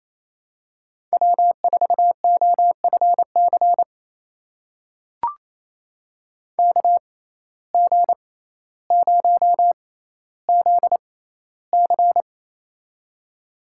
Morse code, heard as a keyed tone, about 28 wpm, spaced as for 14 wpm, 700 Hz. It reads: W4OFC KG0ZC
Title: USA Call Signs - 5 Characters - 6x Character Spacing - 28wpm